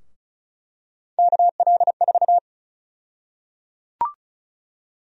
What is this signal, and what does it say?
Morse code, heard as a keyed tone, about 34 wpm, 700 Hz. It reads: KL4
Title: USA Call Sign Prefixes - 35wpm